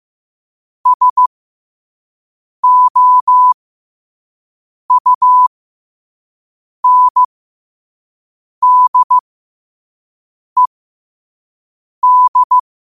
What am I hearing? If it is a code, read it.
Morse code, heard as a keyed tone, about 15 wpm, spaced as for 5 wpm, 1000 Hz. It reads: SOUNDED
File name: qotdfiles / 5wpm